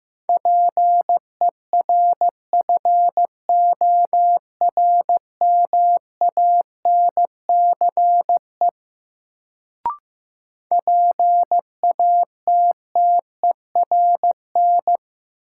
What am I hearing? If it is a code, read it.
Morse code, heard as a keyed tone, about 15 wpm, 700 Hz. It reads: PERFORMANCE PATTERN
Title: Sentences from Top 1000 Words - New Word Drill - 15wpm